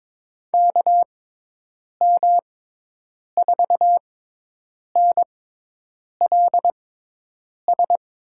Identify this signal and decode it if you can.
Morse code, heard as a keyed tone, about 22 wpm, spaced as for 8 wpm, 700 Hz. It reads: KM4NLS
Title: USA Call Signs - 6 Characters - 6x Character Spacing - 22wpm